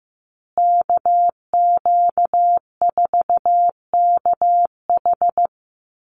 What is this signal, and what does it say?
Morse code, heard as a keyed tone, about 15 wpm, 700 Hz. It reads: KQ4KH
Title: USA Call Signs Repeated 2x 15wpm